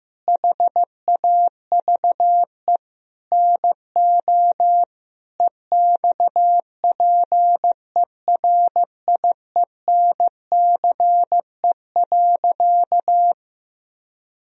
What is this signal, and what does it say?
Morse code, heard as a keyed tone, about 15 wpm, 700 Hz. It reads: HAVE NO EXPERIENCE.
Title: Sentences from Top 1000 Words - Review Part 3 - 15wpm